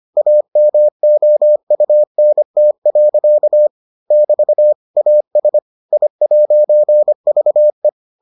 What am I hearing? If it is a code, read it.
Morse code, heard as a keyed tone, about 25 wpm, 600 Hz. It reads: AMOUNT. <BT> AS I'VE